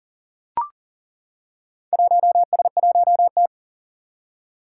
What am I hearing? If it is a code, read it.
Morse code, heard as a keyed tone, about 40 wpm, 700 Hz. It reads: 1S1T
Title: Four Challenging Letter-Number Combinations 40wpm